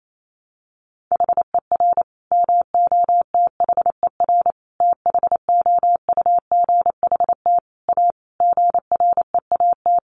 Morse code, heard as a keyed tone, about 28 wpm, 700 Hz. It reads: HER MOTHER THOUGHT A GREAT
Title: Sentences from Top 800 Words 28wpm